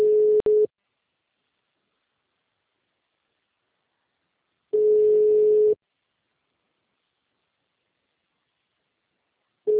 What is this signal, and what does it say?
Ringback tone (Europe, 425 Hz).
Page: Centro - RJ